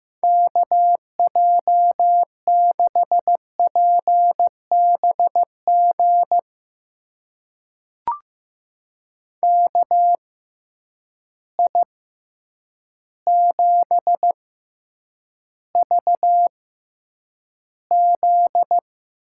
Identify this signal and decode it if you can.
Morse code, heard as a keyed tone, about 15 wpm, spaced as for 8 wpm, 700 Hz. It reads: KJ6PBG KI7VZ
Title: USA Call Signs - 6 Characters - 6x Character Spacing - 15wpm